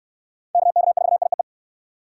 Morse code, heard as a keyed tone, about 69 wpm, 700 Hz. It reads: KK4IEE